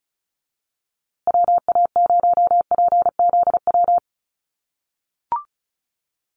Morse code, heard as a keyed tone, about 34 wpm, 700 Hz. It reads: WA0PZW
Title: USA Call Signs Repeated 3x 35wpm